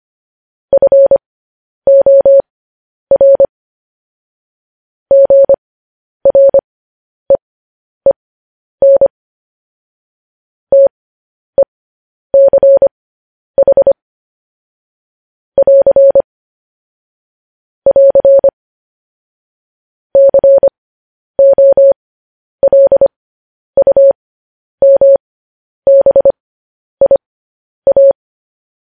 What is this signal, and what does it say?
Morse code, heard as a keyed tone, about 25 wpm, spaced as for 10 wpm, 550 Hz. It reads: FOR GREEN TECH <AR> <AR> COLUMBIA
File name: podcast / News headlines-1x-25x10